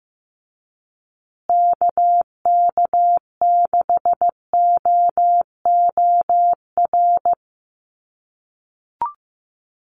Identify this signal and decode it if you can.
Morse code, heard as a keyed tone, about 15 wpm, 700 Hz. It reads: KK6OOR
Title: USA Call Signs Repeated 3x 15wpm